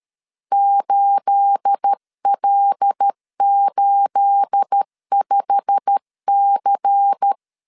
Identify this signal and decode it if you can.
Morse code, heard as a keyed tone, about 13 wpm, 800 Hz. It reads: 8L85C
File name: LEZIONI / 9 28 12wpm 2351